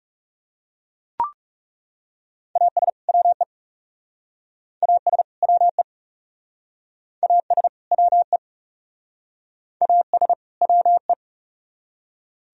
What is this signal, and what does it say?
Morse code, heard as a keyed tone, about 39 wpm, 700 Hz. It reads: AS WE AS WE AS WE AS WE